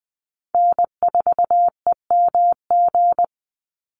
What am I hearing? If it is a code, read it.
Morse code, heard as a keyed tone, about 19 wpm, 700 Hz. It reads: N4EMG
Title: NA SOTA Chaser Callsigns 20wpm